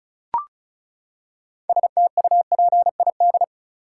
Morse code, heard as a keyed tone, about 36 wpm, 700 Hz. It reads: STUPID
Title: Six Letter Words - 35wpm